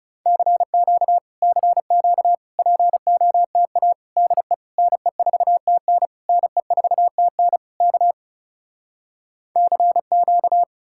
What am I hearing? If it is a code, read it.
Morse code, heard as a keyed tone, about 37 wpm, 700 Hz. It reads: CQ CQ POTA DE NE4TN NE4TN K CQ
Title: Speed-Racing Common QSO Phrases Encore - 30wpm